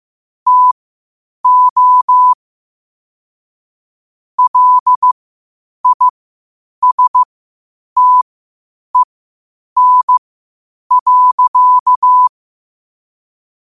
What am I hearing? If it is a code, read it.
Morse code, heard as a keyed tone, about 15 wpm, spaced as for 8 wpm, 1000 Hz. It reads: TO LISTEN.
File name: qotdfiles / 8wpm